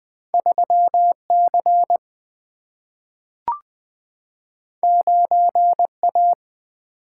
Morse code, heard as a keyed tone, about 20 wpm, 700 Hz. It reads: 3 C 9 A
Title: Single-Letter Single-Number 20wpm